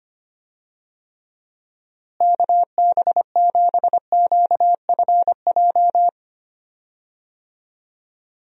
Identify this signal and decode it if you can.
Morse code, heard as a keyed tone, about 25 wpm, 700 Hz. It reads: KB7QFJ
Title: USA Call Signs - 6 Characters - Repeated 2x - 25wpm